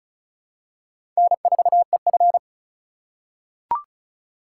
Morse code, heard as a keyed tone, about 34 wpm, 700 Hz. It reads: N4EF
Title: USA Call Signs - 4 Characters - 2x Character Spacing - 35wpm